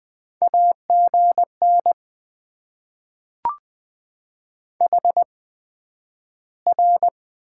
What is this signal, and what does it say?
Morse code, heard as a keyed tone, about 20 wpm, spaced as for 11 wpm, 700 Hz. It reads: AGN HR
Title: Top 90 QSO Elements - 8x Character Spacing - 20wpm